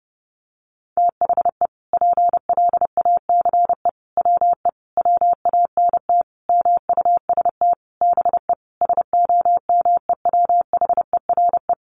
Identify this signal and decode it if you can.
Morse code, heard as a keyed tone, about 28 wpm, 700 Hz. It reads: THE PLACE WE WANT MUST BE SOMEWHERE
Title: Sentences from Top 1400 Words 30wpm